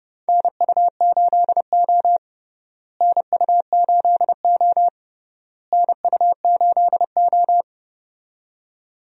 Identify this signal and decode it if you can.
Morse code, heard as a keyed tone, about 31 wpm, 700 Hz. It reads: NU8O NU8O NU8O
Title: USA Call Signs - 4 Characters - Repeated 3x - 30wpm